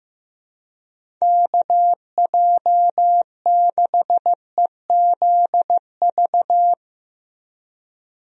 Morse code, heard as a keyed tone, about 15 wpm, 700 Hz. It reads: KJ6EZV
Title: USA Call Signs - 6 Characters - Repeated 2x - 15wpm